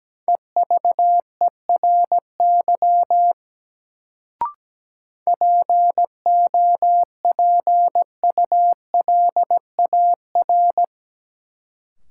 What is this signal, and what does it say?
Morse code, heard as a keyed tone, about 17 wpm, 700 Hz. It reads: EVERY POPULAR
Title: Sets of 5 Words - New Word Drill - 17wpm